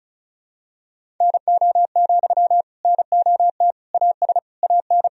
Morse code, heard as a keyed tone, about 36 wpm, 700 Hz. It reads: NO, NOT AS AN
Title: Sentences from Top 1000 Words - Review Part 4 - 35wpm